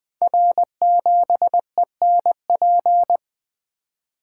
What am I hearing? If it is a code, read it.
Morse code, heard as a keyed tone, about 20 wpm, 700 Hz. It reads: R7ENP
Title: Code Groups - 5 Letter:Numbers - 20wpm